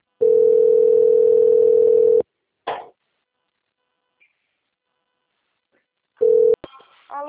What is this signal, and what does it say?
Ringback tone (North America), 440 + 480 Hz.